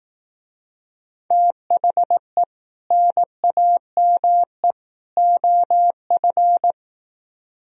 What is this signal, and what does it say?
Morse code, heard as a keyed tone, about 18 wpm, 700 Hz. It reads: THE NAME OF
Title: Speed-Racing - Sets of 3 Words 15wpm